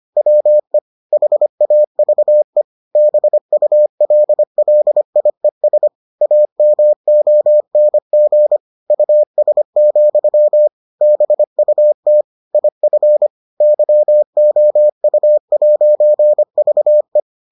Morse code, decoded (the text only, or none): WE HAVE BULLIES AMONG US, BUT IF YOU'VE